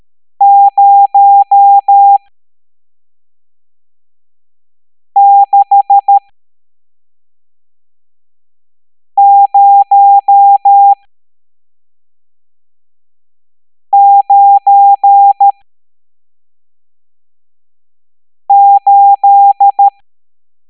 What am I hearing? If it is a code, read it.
Morse code, heard as a keyed tone, about 13 wpm, spaced as for 3 wpm, 800 Hz. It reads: 06098
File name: hamdata / L11-EX01-67890